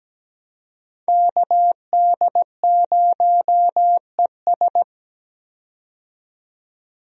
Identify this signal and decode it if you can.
Morse code, heard as a keyed tone, about 17 wpm, 700 Hz. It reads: KD0ES
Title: USA Call Signs - 5 Characters - 17wpm